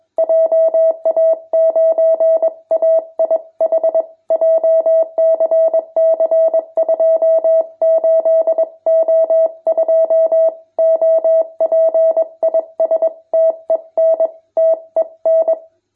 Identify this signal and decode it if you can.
Morse code, heard as a keyed tone, about 21 wpm, 650 Hz. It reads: JA9AIH JCC 28O2 OP IS TEN TEN
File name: jh8cbh / 2021cw